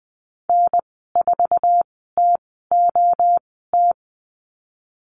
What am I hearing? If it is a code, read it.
Morse code, heard as a keyed tone, about 20 wpm, spaced as for 14 wpm, 700 Hz. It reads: N4TOT
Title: 055 Lesson 10g - Letter r - Call Signs 20wpm